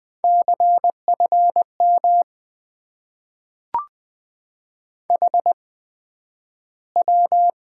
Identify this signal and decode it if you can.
Morse code, heard as a keyed tone, about 20 wpm, spaced as for 11 wpm, 700 Hz. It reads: CFM HW
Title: Top 90 QSO Elements - 8x Character Spacing - 20wpm